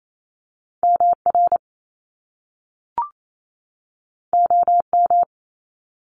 Morse code, heard as a keyed tone, about 28 wpm, 700 Hz. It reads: MR OM
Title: Top Two Letter Combinations in Words 28wpm